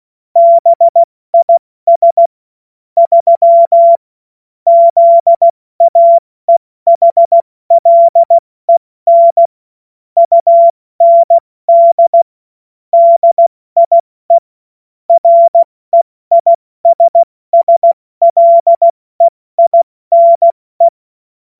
Morse code, heard as a keyed tone, about 16 wpm, 680 Hz. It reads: BIS 3 ZAEHLEN UND DIE REISSLEINE
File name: Morsekurs / Witze / Witz 07. Dezember 2021, 70 F80